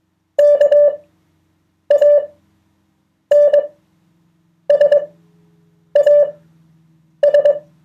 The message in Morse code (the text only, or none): KANSAS